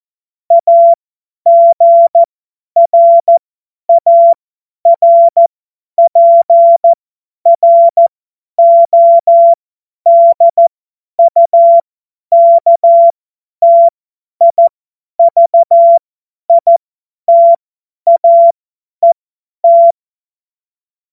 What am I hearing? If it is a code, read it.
Morse code, heard as a keyed tone, about 14 wpm, spaced as for 10 wpm, 680 Hz. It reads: AGRARPRODUKTIVITAET
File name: Morsekurs / Witze / Witz 04. Mai 2021, 50 F70